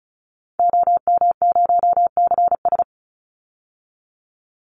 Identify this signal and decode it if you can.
Morse code, heard as a keyed tone, about 34 wpm, 700 Hz. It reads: OM0CS
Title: Call Signs Repeated 3x - 35wpm